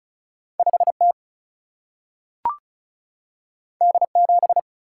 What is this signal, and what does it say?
Morse code, heard as a keyed tone, about 36 wpm, 700 Hz. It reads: 5T D7
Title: Single-Letter Single-Number 35wpm